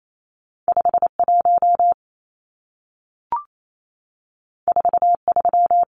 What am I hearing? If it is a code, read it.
Morse code, heard as a keyed tone, about 28 wpm, 700 Hz. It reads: 51 43
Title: Numbers - 2-Digit 28wpm